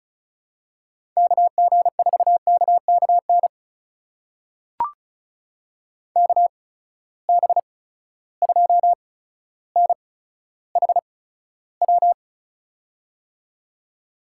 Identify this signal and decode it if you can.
Morse code, heard as a keyed tone, about 36 wpm, spaced as for 15 wpm, 700 Hz. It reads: KG4KKN KB2NHW
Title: USA Call Signs - 6 Characters - 8x Character Spacing - 35wpm